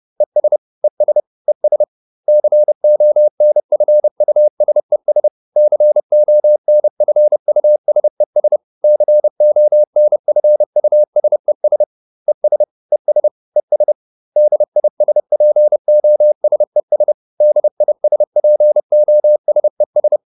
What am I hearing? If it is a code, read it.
Morse code, heard as a keyed tone, about 30 wpm, 600 Hz. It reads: ES ES ES CONFUSES CONFUSES CONFUSES ES ES ES DISPOSES DISPOSES